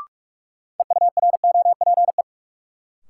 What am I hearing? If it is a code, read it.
Morse code, heard as a keyed tone, about 42 wpm, 700 Hz. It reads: EUROPE
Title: Sets of 5 Words Encore - New Word Drill - 45wpm